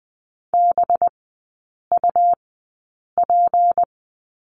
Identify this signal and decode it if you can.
Morse code, heard as a keyed tone, about 20 wpm, spaced as for 8 wpm, 700 Hz. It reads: BUP
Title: 212 Lesson 34e - Number 8 - Three Characters 20wpm